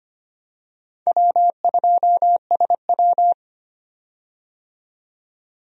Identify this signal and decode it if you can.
Morse code, heard as a keyed tone, about 25 wpm, 700 Hz. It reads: W2SW
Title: USA Call Signs - 4 Characters - 25wpm